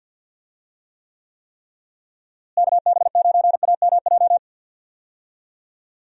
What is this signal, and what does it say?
Morse code, heard as a keyed tone, about 47 wpm, 700 Hz. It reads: KB9AMJ
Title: USA Call Signs - 6 Characters - Repeated 3x - 50wpm